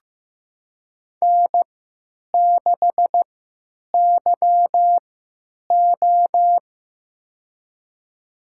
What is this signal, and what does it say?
Morse code, heard as a keyed tone, about 15 wpm, spaced as for 8 wpm, 700 Hz. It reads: N6YO
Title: USA Call Signs - 4 Characters - 3x Character Spacing - 15wpm